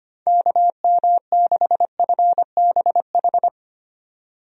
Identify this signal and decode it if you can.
Morse code, heard as a keyed tone, about 25 wpm, 700 Hz. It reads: KM6FBH
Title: USA Call Signs - 6 Characters - 25wpm